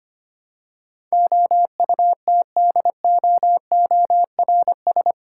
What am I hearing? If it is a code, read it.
Morse code, heard as a keyed tone, about 25 wpm, 700 Hz. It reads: OUTDOORS
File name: 8 Letter Words - First 3 Letters Given - 25wpm